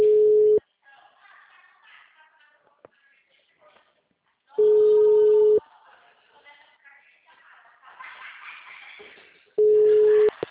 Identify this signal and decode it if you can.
Ringback tone (Europe, 425 Hz), with speech.
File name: out- -2014- - - .56380